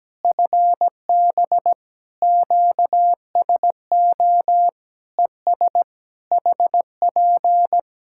Morse code, heard as a keyed tone, about 17 wpm, 700 Hz. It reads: FB QSO ES HP
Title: Practice QSO - Exchange 4 - Ending - 17wpm